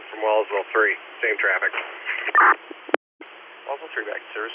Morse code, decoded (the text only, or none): none